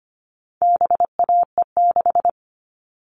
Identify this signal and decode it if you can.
Morse code, heard as a keyed tone, about 25 wpm, 700 Hz. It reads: BAE6
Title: All Cut Numbers - 4-Digit 25wpm